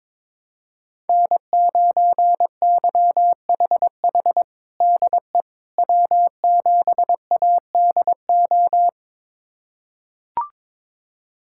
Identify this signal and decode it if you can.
Morse code, heard as a keyed tone, about 22 wpm, 700 Hz. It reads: N9YHH DE W7ADO
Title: USA Callsign DE Callsign - 5 Characters - 22wpm